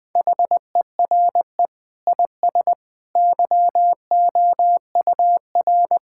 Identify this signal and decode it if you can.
Morse code, heard as a keyed tone, about 20 wpm, 700 Hz. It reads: HERE IS YOUR
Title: Sentences from Top 1600 Words 20wpm